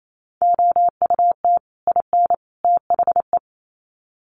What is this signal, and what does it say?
Morse code, heard as a keyed tone, about 28 wpm, 700 Hz. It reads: OUT IN THE
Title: Sets of 3 Words 28wpm